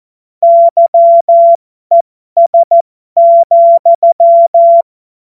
Morse code, heard as a keyed tone, about 14 wpm, spaced as for 12 wpm, 680 Hz. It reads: YES,
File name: Morsekurs / Witze